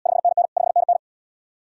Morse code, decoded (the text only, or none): 5NN 5NN